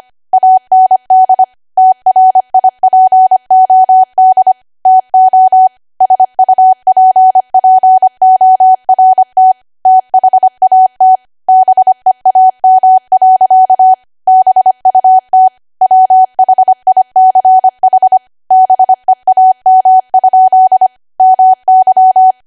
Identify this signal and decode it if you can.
Morse code, heard as a keyed tone, about 25 wpm, 750 Hz. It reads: AND TRIPOD TO SUPPORT THAT BEAM. BUT WHICH BEAM? MY
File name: Archive / 25 WPM / 25WPM